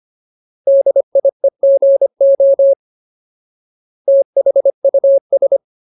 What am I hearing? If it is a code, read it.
Morse code, heard as a keyed tone, about 25 wpm, 550 Hz. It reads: DIEGO THUS
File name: Ultimate Word List / 25WPM Ultimate